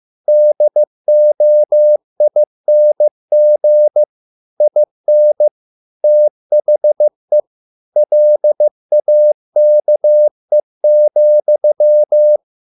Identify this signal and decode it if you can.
Morse code, heard as a keyed tone, about 15 wpm, 600 Hz. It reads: DOING IN THE LAKE,